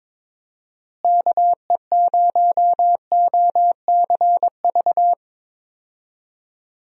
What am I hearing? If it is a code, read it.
Morse code, heard as a keyed tone, about 22 wpm, 700 Hz. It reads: KE0OCV
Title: USA Call Signs - 6 Characters - Repeated 2x - 22wpm